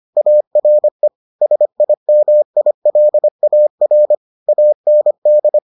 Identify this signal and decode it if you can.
Morse code, heard as a keyed tone, about 25 wpm, 600 Hz. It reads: ARE SIMILAR AND